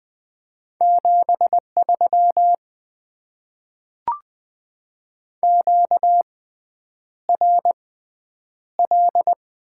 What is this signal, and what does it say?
Morse code, heard as a keyed tone, about 20 wpm, spaced as for 9 wpm, 690 Hz. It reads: 73 QRL
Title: Top 20 QSO Elements - 6x Character Spacing - 20wpm